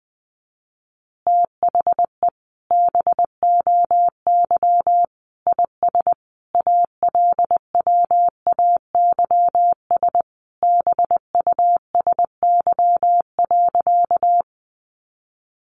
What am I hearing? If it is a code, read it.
Morse code, heard as a keyed tone, about 20 wpm, 700 Hz. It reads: THE BOY IS ALWAYS BUSY.
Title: Additional Easy Phrases 20wpm